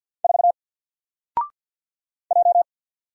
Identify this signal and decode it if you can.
Morse code, heard as a keyed tone, about 50 wpm, 700 Hz. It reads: 4 J
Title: Single Letter-Number Alt 50wpm